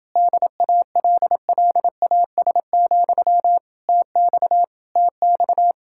Morse code, heard as a keyed tone, about 28 wpm, 700 Hz. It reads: DALLAS, TX TX